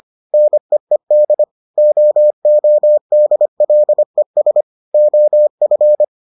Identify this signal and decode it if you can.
Morse code, heard as a keyed tone, about 25 wpm, 600 Hz. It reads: NEED OODLES OF